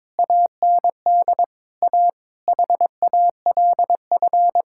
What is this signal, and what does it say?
Morse code, heard as a keyed tone, about 22 wpm, 700 Hz. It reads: AND A HALF
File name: practice / 22wpm / Sentences with 3-Word Phrases - 22wpm